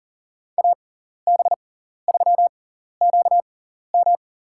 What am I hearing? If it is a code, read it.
Morse code, heard as a keyed tone, about 40 wpm, spaced as for 14 wpm, 700 Hz. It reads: AB3QM
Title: USA Call Signs - 5 Characters - 6x Character Spacing - 40wpm